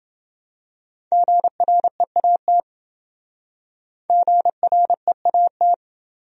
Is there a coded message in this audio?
Yes: Morse code from 1.1 to 5.7 s, heard as a keyed tone, about 27 wpm, 700 Hz. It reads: GREAT GREAT